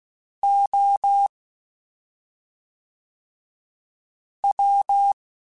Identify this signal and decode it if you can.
Morse code, heard as a keyed tone, about 16 wpm, spaced as for 3 wpm, 800 Hz. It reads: OW